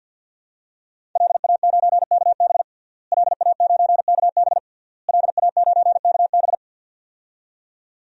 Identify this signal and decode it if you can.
Morse code, heard as a keyed tone, about 53 wpm, 700 Hz. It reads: LA9KB LA9KB LA9KB
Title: DXCC-100 Call Signs - 5 Characters - Repeated 3x - 50wpm